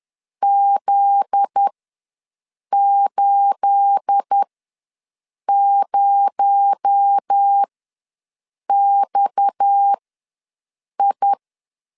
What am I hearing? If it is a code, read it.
Morse code, heard as a keyed tone, about 10 wpm, spaced as for 6 wpm, 800 Hz. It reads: Z80XI
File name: LEZIONI / 10 SIGNS / 10A SLASH ws 10wpm 4121